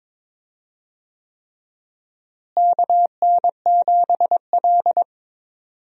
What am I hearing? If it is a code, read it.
Morse code, heard as a keyed tone, about 23 wpm, 700 Hz. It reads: KN7L
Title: USA Call Signs - 4 Characters - Repeated 3x - 22wpm